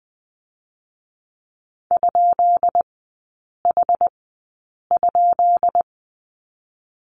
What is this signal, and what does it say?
Morse code, heard as a keyed tone, about 20 wpm, spaced as for 8 wpm, 690 Hz. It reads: ?H?
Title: 133 Lesson 22e - Question Mark - Three Characters 20wpm